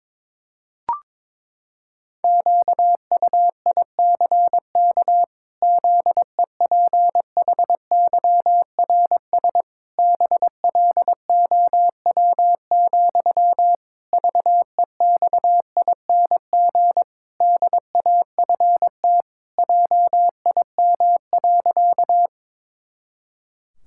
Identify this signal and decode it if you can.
Morse code, heard as a keyed tone, about 22 wpm, 700 Hz. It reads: QUICK ZEPHYRS BLOW, VEXING DAFT JIM.
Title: Pangram Challenge v2 - 22wpm